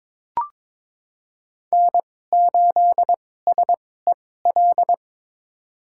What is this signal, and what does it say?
Morse code, heard as a keyed tone, about 22 wpm, spaced as for 16 wpm, 700 Hz. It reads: N8SEL
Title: USA Call Signs - 5 Characters - 2x Character Spacing - 22wpm